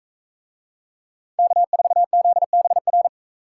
Morse code, heard as a keyed tone, about 40 wpm, 700 Hz. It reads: K4ZBR